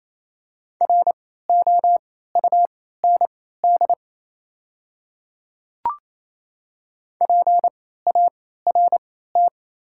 Morse code, heard as a keyed tone, about 28 wpm, spaced as for 16 wpm, 700 Hz. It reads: ROUND PART
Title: 3x Character Spacing - Top 200 Words - 28wpm